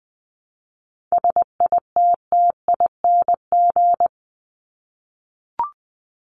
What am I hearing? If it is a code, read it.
Morse code, heard as a keyed tone, about 20 wpm, 700 Hz. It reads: SITTING
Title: ING Suffix - Top 100 Words with Base Words 20wpm